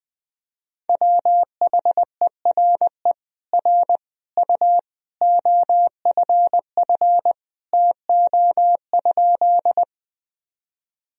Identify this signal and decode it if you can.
Morse code, heard as a keyed tone, about 21 wpm, 700 Hz. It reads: WHERE R U OFF TO?
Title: Sentences from Top 200 Words and Abbreviations 20wpm